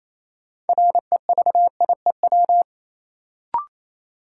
Morse code, heard as a keyed tone, about 28 wpm, 700 Hz. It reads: REVIEW
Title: Sentences from Top 900 Words - New Word Drill - 28wpm